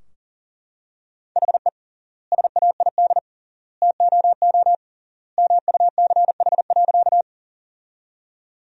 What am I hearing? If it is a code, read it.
Morse code, heard as a keyed tone, about 40 wpm, 700 Hz. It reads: HE SAID TOO MUCH.
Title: 3x Word-Spacing - Sentences from Top 200 Words - 40wpm